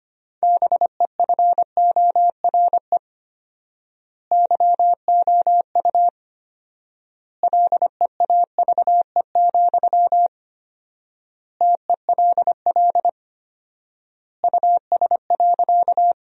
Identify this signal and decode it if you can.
Morse code, heard as a keyed tone, about 25 wpm, 700 Hz. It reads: BEFORE YOU LEAVE, TELL US.
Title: 4x Word-Spacing - Sentences from Top 400 Words - 25wpm